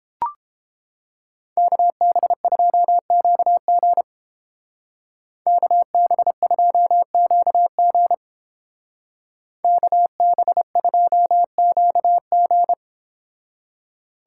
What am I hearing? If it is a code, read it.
Morse code, heard as a keyed tone, about 29 wpm, 700 Hz. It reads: KB2QG KB2QG KB2QG